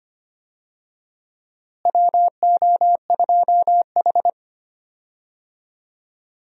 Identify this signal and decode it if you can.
Morse code, heard as a keyed tone, about 25 wpm, 700 Hz. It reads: WO2H
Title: USA Call Signs - 4 Characters - Repeated 3x - 25wpm